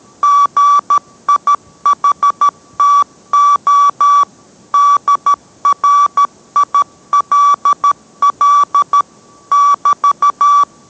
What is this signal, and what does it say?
Morse code, heard as a keyed tone, about 13 wpm, 1150 Hz. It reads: GIHTO DRILL <BT>